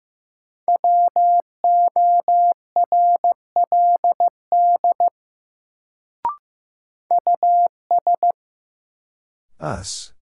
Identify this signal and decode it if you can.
Morse code, heard as a keyed tone, about 15 wpm, 700 Hz. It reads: WORLD US
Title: Sets of 2 Words - New Word Drill - 15wpm